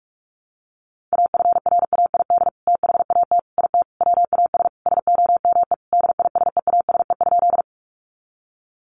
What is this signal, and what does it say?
Morse code, heard as a keyed tone, about 44 wpm, 700 Hz. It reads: AFRAID THAT IT WAS SOME DISEASE?
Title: Sentences from Top 1500 Words 45wpm